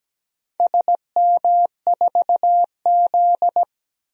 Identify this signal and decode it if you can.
Morse code, heard as a keyed tone, about 17 wpm, 700 Hz. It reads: SM4Z
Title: DXCC-100 Call Signs - 4 Characters - 17wpm